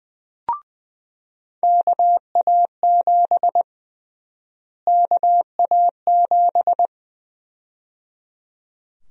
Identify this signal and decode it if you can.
Morse code, heard as a keyed tone, about 20 wpm, 700 Hz. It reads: KA7 KA7
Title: USA Call Sign Prefixes - Repeated 2x - 20wpm